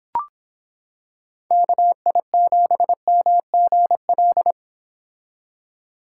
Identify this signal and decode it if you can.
Morse code, heard as a keyed tone, about 26 wpm, 700 Hz. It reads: KI7MGL